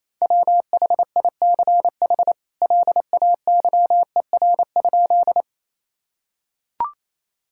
Morse code, heard as a keyed tone, about 28 wpm, 700 Hz. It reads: WHICH LAYER?
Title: Sentences from Top 1600 Words 28wpm